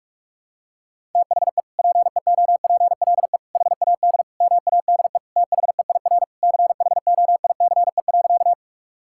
Morse code, heard as a keyed tone, about 46 wpm, 700 Hz. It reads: THE PEOPLE HAD MADE THEIR CHOICE.
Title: Sentences from Top 1000 Words - Review Part 6 - 45wpm